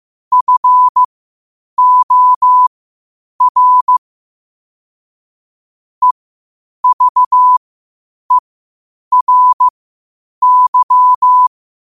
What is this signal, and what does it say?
Morse code, heard as a keyed tone, about 15 wpm, spaced as for 8 wpm, 1000 Hz. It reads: FOR EVERY
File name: qotdfiles / 8wpm